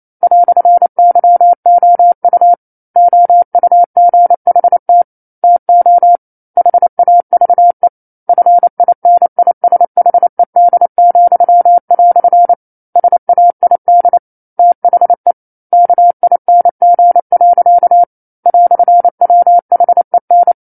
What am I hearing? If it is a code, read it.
Morse code, heard as a keyed tone, about 28 wpm, 700 Hz. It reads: "YOU OUGHT TO HAVE FINISHED," SAID THE KING. "WHEN
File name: 09-Chapter-09 28 WPM